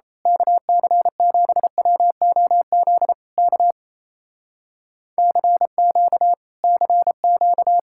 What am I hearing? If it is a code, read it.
Morse code, heard as a keyed tone, about 30 wpm, 700 Hz. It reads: KC7WOZ K CQ CQ
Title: Speed-Racing Common QSO Phrases - 28wpm